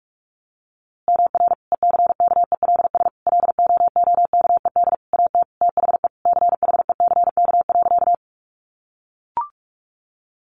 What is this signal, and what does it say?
Morse code, heard as a keyed tone, about 44 wpm, 700 Hz. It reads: MR ECKELS LOOKED AT THE CHECK.
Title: A Sound Of Thunder 45wpm